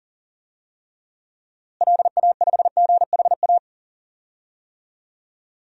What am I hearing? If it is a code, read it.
Morse code, heard as a keyed tone, about 40 wpm, 700 Hz. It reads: LA5GHA
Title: DXCC-100 Call Signs - 6 Characters - Repeated 3x - 40wpm